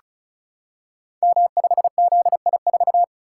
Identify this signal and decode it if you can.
Morse code, heard as a keyed tone, about 36 wpm, 700 Hz. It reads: M5ZI4
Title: Code Groups - 5 Letter:Numbers - 35wpm